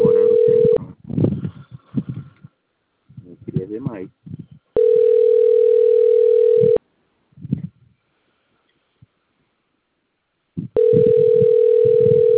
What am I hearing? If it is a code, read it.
Ringback tone (North America), 440 + 480 Hz, with speech.
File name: out- - - - - .2633